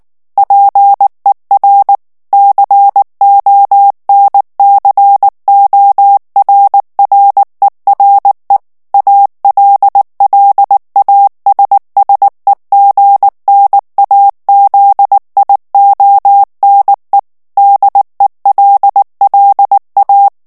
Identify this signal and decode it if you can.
Morse code, heard as a keyed tone, about 19 wpm, 800 Hz. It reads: PER CONCORRERE ALLASSEGNAZIONE DELLA
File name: morse mp3 / ITALIEN texte clair 2 vitesse 18 mots minute